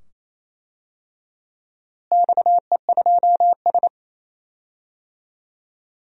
Morse code, heard as a keyed tone, about 28 wpm, 700 Hz. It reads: XE2S
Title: DXCC-100 Call Signs - 4 Characters - Repeated 2x - 28wpm